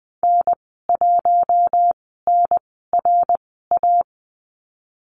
Morse code, heard as a keyed tone, about 20 wpm, spaced as for 15 wpm, 700 Hz. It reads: N1NRA
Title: 082 Lesson 14g - Number 2 - Call Signs 20wpm